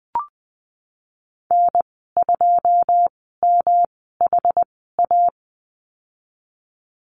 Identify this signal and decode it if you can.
Morse code, heard as a keyed tone, about 20 wpm, spaced as for 15 wpm, 700 Hz. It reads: N2MHA
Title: 122 Lesson 20g - Number 3 - Call Signs 20wpm